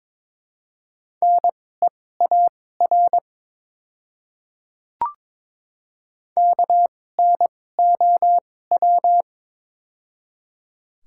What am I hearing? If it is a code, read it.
Morse code, heard as a keyed tone, about 23 wpm, spaced as for 16 wpm, 700 Hz. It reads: NEAR KNOW
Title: 2x Character Spacing - Top 200 Words - 22wpm